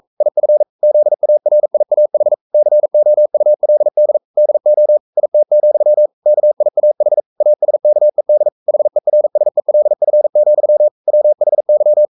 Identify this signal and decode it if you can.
Morse code, heard as a keyed tone, about 44 wpm, 600 Hz. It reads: IF ZARIAH COULD DO IT, KIAH ASKED HERSELF, WHY
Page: Romanen Never af Ken Follet på engelsk.